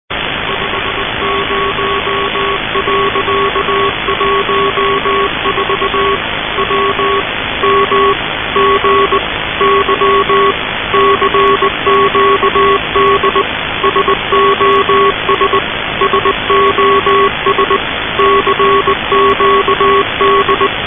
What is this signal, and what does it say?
Morse code, heard as a keyed tone, about 19 wpm, 1110 Hz. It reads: 50.14 W MGY CQD SOS SOS CQD
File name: luxorion / Radio / titanic-21wpm